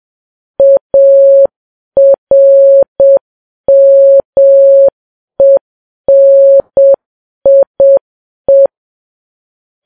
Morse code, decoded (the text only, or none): ARMENIE